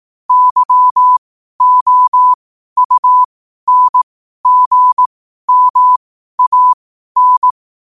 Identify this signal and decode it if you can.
Morse code, heard as a keyed tone, about 18 wpm, spaced as for 12 wpm, 1000 Hz. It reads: YOUNGMAN
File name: qotdfiles / 12wpm